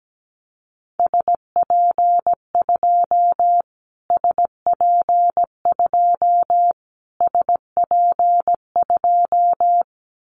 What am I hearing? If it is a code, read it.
Morse code, heard as a keyed tone, about 17 wpm, 700 Hz. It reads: SP2 SP2 SP2
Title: Callsign Prefixes for Top 25 DXCC - Repeated 3x - 17wpm